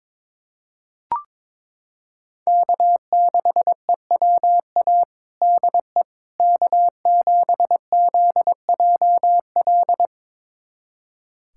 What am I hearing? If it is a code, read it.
Morse code, heard as a keyed tone, about 22 wpm, 700 Hz. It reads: K6EWA DE K7ZJL
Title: USA Callsign DE Callsign - 5 Characters - 22wpm